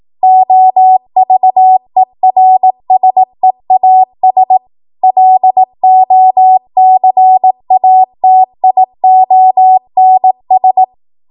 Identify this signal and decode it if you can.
Morse code, heard as a keyed tone, about 18 wpm, 750 Hz. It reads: OVERSEAS LOCATIONS
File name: Archive / 18 WPM / 18WPM